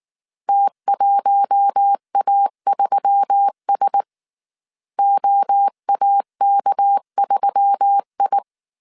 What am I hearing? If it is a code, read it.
Morse code, heard as a keyed tone, about 19 wpm, 800 Hz. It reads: T1A3S OAK3I